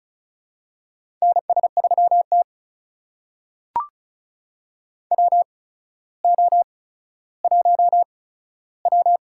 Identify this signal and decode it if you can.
Morse code, heard as a keyed tone, about 36 wpm, spaced as for 15 wpm, 700 Hz. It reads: NS3T WO1W
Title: USA Call Signs - 4 Characters - 8x Character Spacing - 35wpm